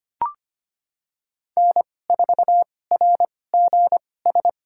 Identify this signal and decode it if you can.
Morse code, heard as a keyed tone, about 25 wpm, spaced as for 18 wpm, 700 Hz. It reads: N4RGS
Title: USA Call Signs - 5 Characters - 2x Character Spacing - 25wpm